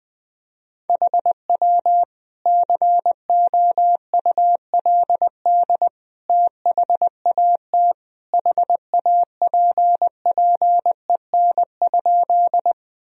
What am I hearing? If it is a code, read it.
Morse code, heard as a keyed tone, about 20 wpm, 700 Hz. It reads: HW COULD THAT HAPPEN?
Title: Sentences from Top 400 Words and Abbreviations 20wpm